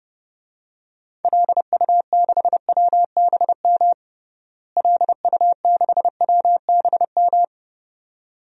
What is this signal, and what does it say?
Morse code, heard as a keyed tone, about 30 wpm, 700 Hz. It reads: LU6WBM LU6WBM
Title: DXCC-100 Call Signs - 6 Characters - Repeated 2x - 30wpm